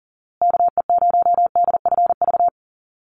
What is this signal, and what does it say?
Morse code, heard as a keyed tone, about 40 wpm, 700 Hz. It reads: KE0DFV
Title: USA Call Signs Repeated 3x 40wpm